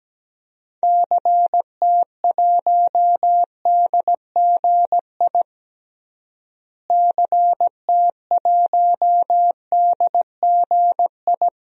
Morse code, heard as a keyed tone, about 17 wpm, 700 Hz. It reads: CT1DGI CT1DGI
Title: DXCC-100 Call Signs - 6 Characters - Repeated 2x - 17wpm